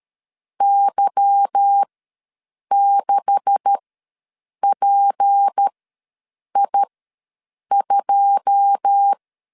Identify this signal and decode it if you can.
Morse code, heard as a keyed tone, about 13 wpm, spaced as for 7 wpm, 800 Hz. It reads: Y6PI2
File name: LEZIONI / 10 SIGNS / 10D COMA ws 12wpm 3618